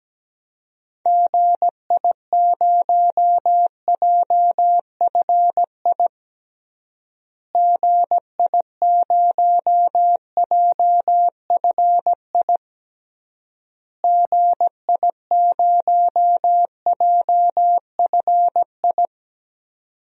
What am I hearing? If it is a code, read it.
Morse code, heard as a keyed tone, about 17 wpm, 700 Hz. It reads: GI0JFI GI0JFI GI0JFI
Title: DXCC-100 Call Signs - 6 Characters - Repeated 3x - 17wpm